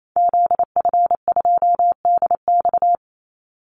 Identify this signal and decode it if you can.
Morse code, heard as a keyed tone, about 28 wpm, 700 Hz. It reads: ZF2DX
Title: Call Signs - 28wpm